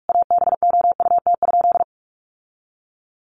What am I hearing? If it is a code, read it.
Morse code, heard as a keyed tone, about 44 wpm, 700 Hz. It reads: ABOUT?
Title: Sentences from Top 1100 Words 45wpm